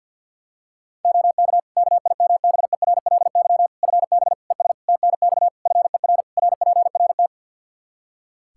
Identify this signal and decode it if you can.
Morse code, heard as a keyed tone, about 52 wpm, 700 Hz. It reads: OK KIMBERLY FB ES TNX FER RPRT
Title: Practice QSO - Exchange 2 - Info - 50wpm